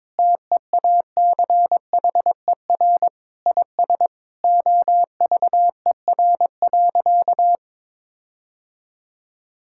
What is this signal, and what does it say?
Morse code, heard as a keyed tone, about 22 wpm, 700 Hz. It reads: TEACHER IS OVER.
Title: Sentences from Top 1600 Words 22wpm